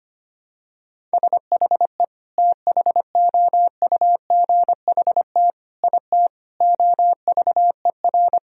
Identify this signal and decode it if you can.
Morse code, heard as a keyed tone, about 25 wpm, 700 Hz. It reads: SHE THOUGHT IT OVER
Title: Sentences from Top 1000 Words - Review Part 1 - 25wpm